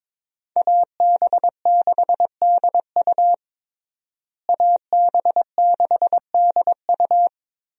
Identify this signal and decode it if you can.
Morse code, heard as a keyed tone, about 22 wpm, 700 Hz. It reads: AB6DU AB6DU
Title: USA Call Signs - 5 Characters - Repeated 2x - 22wpm